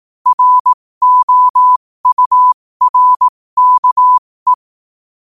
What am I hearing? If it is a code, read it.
Morse code, heard as a keyed tone, about 18 wpm, spaced as for 15 wpm, 1000 Hz. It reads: ROURKE
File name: qotdfiles / 15wpm